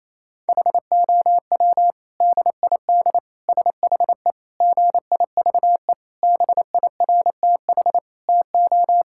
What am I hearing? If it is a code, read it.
Morse code, heard as a keyed tone, about 28 wpm, 700 Hz. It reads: HOW DID SHE GIVE BIRTH TO
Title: Sentences from Top 1000 Words - Review Part 5 - 28wpm